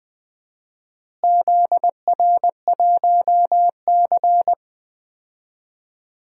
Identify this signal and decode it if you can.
Morse code, heard as a keyed tone, about 20 wpm, 700 Hz. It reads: ZR1C
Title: DXCC-100 Call Signs - 4 Characters - Repeated 3x - 20wpm